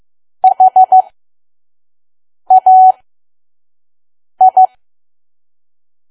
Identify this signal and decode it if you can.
Morse code, heard as a keyed tone, about 15 wpm, spaced as for 5 wpm, 750 Hz. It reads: HAI
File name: Archive / 5 WPM files / PM